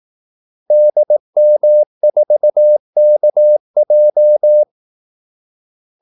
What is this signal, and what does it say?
Morse code, heard as a keyed tone, about 18 wpm, 600 Hz. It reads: DM4KJ
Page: Kaldesignaler 18 wpm